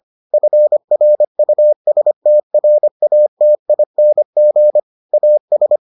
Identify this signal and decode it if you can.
Morse code, heard as a keyed tone, about 25 wpm, 600 Hz. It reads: FRUSTRATING AS